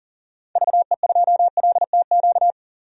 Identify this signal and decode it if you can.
Morse code, heard as a keyed tone, about 40 wpm, 700 Hz. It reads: VE2LTQ
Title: DXCC-100 Call Signs - 6 Characters - 40wpm